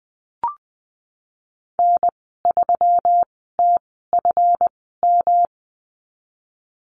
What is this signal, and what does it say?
Morse code, heard as a keyed tone, about 20 wpm, spaced as for 15 wpm, 700 Hz. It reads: N3TFM
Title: 154 Lesson 25g - Letter p - Call Signs 20wpm